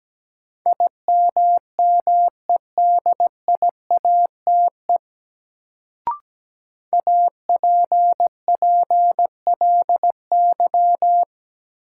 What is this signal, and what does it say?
Morse code, heard as a keyed tone, about 17 wpm, 700 Hz. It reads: IMMEDIATE APPLY
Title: Sentences from Top 1200 Words - New Word Drill Alt - 17wpm